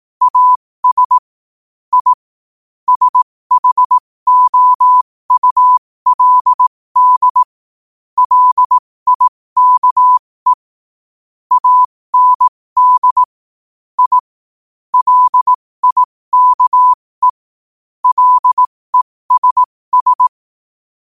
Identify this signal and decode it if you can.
Morse code, heard as a keyed tone, about 18 wpm, spaced as for 15 wpm, 1000 Hz. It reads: AS I SHOULD LIKE AND I LIKE LESS
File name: qotdfiles / 15wpm